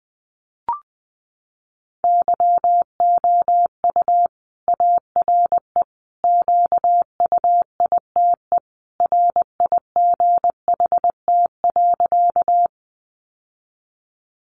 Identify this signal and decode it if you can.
Morse code, heard as a keyed tone, about 20 wpm, 700 Hz. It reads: YOU ARE QUITE RIGHT.
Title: Sentences from Top 700 Words 20wpm